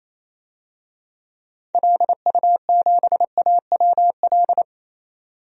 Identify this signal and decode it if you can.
Morse code, heard as a keyed tone, about 28 wpm, 700 Hz. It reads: LU7AWL
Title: DXCC-100 Call Signs - 6 Characters - 28wpm